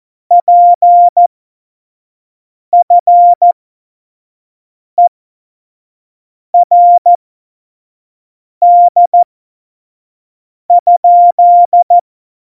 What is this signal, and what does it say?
Morse code, heard as a keyed tone, about 14 wpm, spaced as for 5 wpm, 700 Hz. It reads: PFERD?
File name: Morsekurs / Witze / Witz 14 Juli 2020, 25 F70